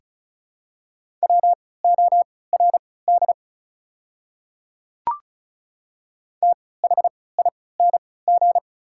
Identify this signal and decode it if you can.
Morse code, heard as a keyed tone, about 36 wpm, spaced as for 20 wpm, 690 Hz. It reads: WORD THING
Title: 3x Character Spacing - Top 100 Words - 35wpm